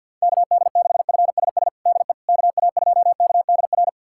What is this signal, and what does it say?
Morse code, heard as a keyed tone, about 52 wpm, 700 Hz. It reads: KD6FSS DE KA2KDR
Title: USA Callsign DE Callsign 50wpm